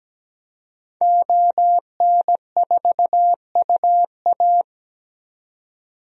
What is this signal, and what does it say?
Morse code, heard as a keyed tone, about 17 wpm, 700 Hz. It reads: ON4UA
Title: DXCC-100 Call Signs - 5 Characters - 17wpm